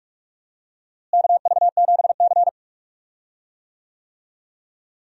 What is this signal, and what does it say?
Morse code, heard as a keyed tone, about 46 wpm, 700 Hz. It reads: KV7C
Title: USA Call Signs - 4 Characters - Repeated 3x - 45wpm